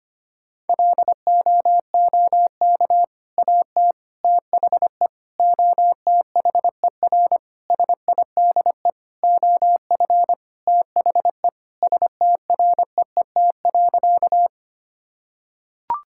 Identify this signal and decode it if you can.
Morse code, heard as a keyed tone, about 25 wpm, 700 Hz. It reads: LOOK AT THE OTHER SIDE OF THE STREET.
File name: practice / 25wpm / Sentences with 3-Word Phrases - 25wpm